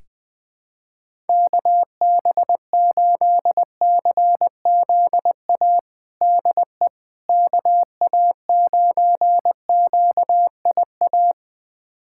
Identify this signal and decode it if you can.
Morse code, heard as a keyed tone, about 21 wpm, 700 Hz. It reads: KB8CZA DE KA9QIA
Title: USA Callsign DE Callsign 20wpm